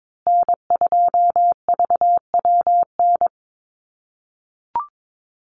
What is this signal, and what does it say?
Morse code, heard as a keyed tone, about 22 wpm, 700 Hz. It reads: N2VWN
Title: USA Call Signs Repeated 3x 22wpm